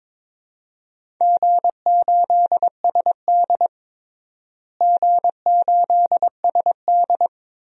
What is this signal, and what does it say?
Morse code, heard as a keyed tone, about 22 wpm, 700 Hz. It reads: G8SD G8SD
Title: DXCC-100 Call Signs - 4 Characters - Repeated 2x - 22wpm